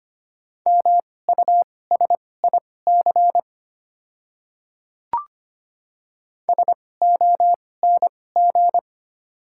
Morse code, heard as a keyed tone, about 25 wpm, spaced as for 18 wpm, 700 Hz. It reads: MUSIC SONG
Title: 2x Character Spacing - Top 500 Words - 25wpm